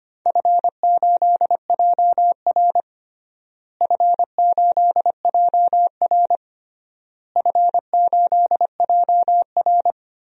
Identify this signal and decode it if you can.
Morse code, heard as a keyed tone, about 25 wpm, 700 Hz. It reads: F8JR F8JR F8JR
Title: DXCC-100 Call Signs - 4 Characters - Repeated 3x - 25wpm